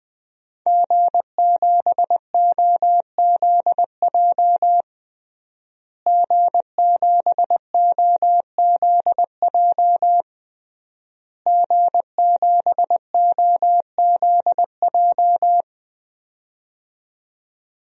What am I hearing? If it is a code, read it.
Morse code, heard as a keyed tone, about 20 wpm, 700 Hz. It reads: G7OZJ G7OZJ G7OZJ
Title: DXCC-100 Call Signs - 5 Characters - Repeated 3x - 20wpm